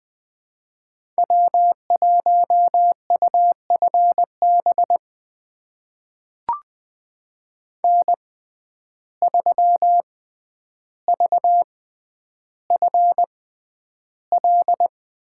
Morse code, heard as a keyed tone, about 20 wpm, spaced as for 10 wpm, 700 Hz. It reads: W1UFB N3VFL
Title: USA Call Signs - 5 Characters - 6x Character Spacing - 20wpm